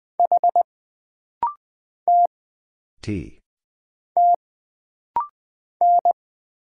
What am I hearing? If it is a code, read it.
Morse code, heard as a keyed tone, about 20 wpm, 700 Hz. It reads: H T T N
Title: Single Letter-Number Alt - Rapid-Fire 20wpm